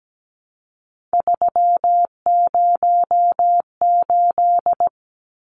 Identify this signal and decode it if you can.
Morse code, heard as a keyed tone, about 17 wpm, 700 Hz. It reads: 308
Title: Numbers - 3-Digit 17wpm